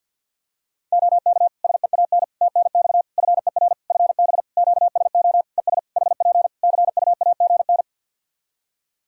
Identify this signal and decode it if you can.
Morse code, heard as a keyed tone, about 52 wpm, 700 Hz. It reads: OK SEAN TNX FER FB QSO ES HP CUAGN
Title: Practice QSO - Exchange 4 - Ending - 50wpm